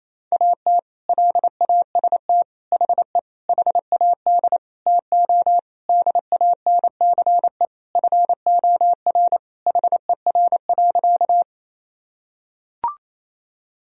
Morse code, heard as a keyed tone, about 28 wpm, 700 Hz. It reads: AT LAST HE HAD TO DANCE FOR HER.
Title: Sentences from Top 1000 Words - Review Part 4 - 28wpm